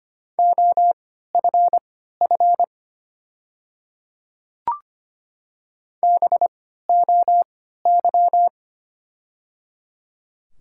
Morse code, heard as a keyed tone, about 25 wpm, spaced as for 14 wpm, 700 Hz. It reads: OFF BOY
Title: 3x Character Spacing - Top 200 Words - 25wpm